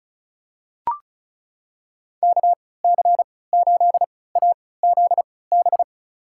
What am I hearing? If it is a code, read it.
Morse code, heard as a keyed tone, about 36 wpm, spaced as for 20 wpm, 700 Hz. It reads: KC8AZB
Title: USA Call Signs - 6 Characters - 3x Character Spacing - 35wpm